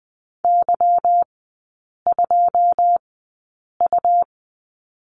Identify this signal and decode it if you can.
Morse code, heard as a keyed tone, about 20 wpm, spaced as for 8 wpm, 690 Hz. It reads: Y2U
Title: 166 Lesson 27e - Number 7 - Three Characters 20wpm